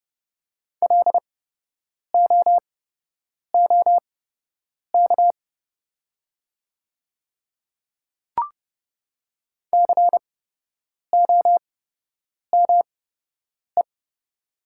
Morse code, heard as a keyed tone, about 31 wpm, spaced as for 8 wpm, 700 Hz. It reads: LOOK COME
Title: 8x Character Spacing - Top 100 Words - 30wpm